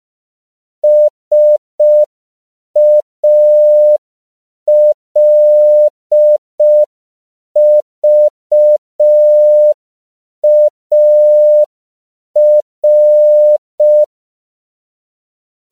Morse code, heard as a keyed tone, about 5 wpm, 600 Hz. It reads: SALVAR
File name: CW / 5PPM